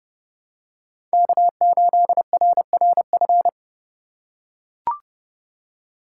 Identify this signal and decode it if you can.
Morse code, heard as a keyed tone, about 30 wpm, 700 Hz. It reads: K8RRF
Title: USA Call Signs - 5 Characters - 2x Character Spacing - 30wpm